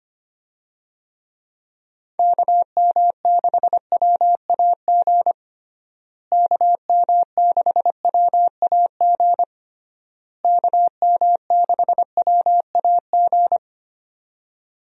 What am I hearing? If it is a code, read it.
Morse code, heard as a keyed tone, about 25 wpm, 700 Hz. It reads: KM6WAG KM6WAG KM6WAG
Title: USA Call Signs - 6 Characters - Repeated 3x - 25wpm